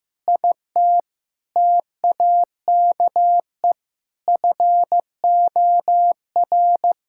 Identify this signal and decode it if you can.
Morse code, heard as a keyed tone, about 16 wpm, 700 Hz. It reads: IT TAKE FOR
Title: Sentences from Top 1600 Words 15wpm